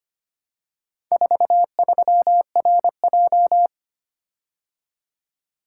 Morse code, heard as a keyed tone, about 25 wpm, 690 Hz. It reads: 43RJ
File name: practice / 25wpm / Four Letter-Number Combinations - 25wpm